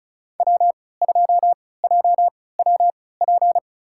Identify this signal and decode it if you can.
Morse code, heard as a keyed tone, about 34 wpm, spaced as for 20 wpm, 700 Hz. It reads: W2JWP
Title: USA Call Signs - 5 Characters - 3x Character Spacing - 35wpm